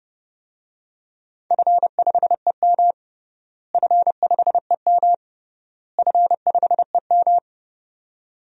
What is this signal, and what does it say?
Morse code, heard as a keyed tone, about 30 wpm, 700 Hz. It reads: F5EM F5EM F5EM
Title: DXCC-100 Call Signs - 4 Characters - Repeated 3x - 30wpm